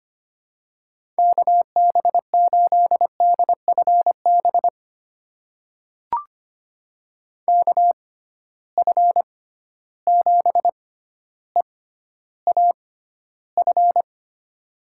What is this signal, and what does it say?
Morse code, heard as a keyed tone, about 25 wpm, spaced as for 13 wpm, 700 Hz. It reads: KB8DFB KF7EAF
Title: USA Call Signs - 6 Characters - 6x Character Spacing - 25wpm